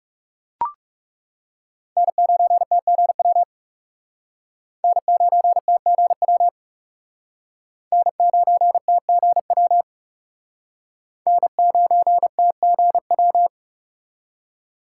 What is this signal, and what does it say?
Morse code, heard as a keyed tone, about 35 wpm, 700 Hz. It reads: N9TGW N9TGW N9TGW N9TGW
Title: Speed Racing - USA 1x3 Call Signs 30wpm